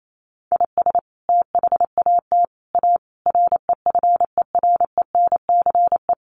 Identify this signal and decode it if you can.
Morse code, heard as a keyed tone, about 28 wpm, 700 Hz. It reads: IS THAT A REFERENCE
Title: Sentences from Top 1200 Words 28wpm